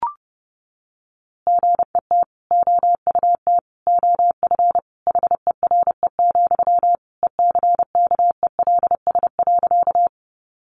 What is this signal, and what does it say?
Morse code, heard as a keyed tone, about 30 wpm, 700 Hz. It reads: GET OUT OF HERE, ECKELS.
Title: A Sound Of Thunder 30wpm